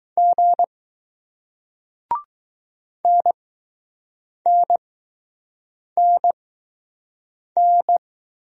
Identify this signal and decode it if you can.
Morse code, heard as a keyed tone, about 20 wpm, spaced as for 6 wpm, 700 Hz. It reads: G NNNN